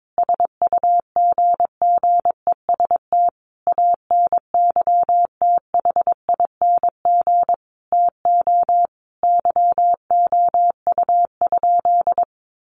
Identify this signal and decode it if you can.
Morse code, heard as a keyed tone, about 22 wpm, 700 Hz. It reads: SUGGEST ANYTHING TO YOU?
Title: Sentences from Top 1300 Words 22wpm